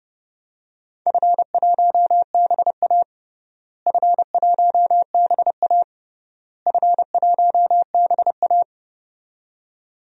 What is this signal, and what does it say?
Morse code, heard as a keyed tone, about 32 wpm, 700 Hz. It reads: F1BA F1BA F1BA
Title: DXCC-100 Call Signs - 4 Characters - Repeated 3x - 30wpm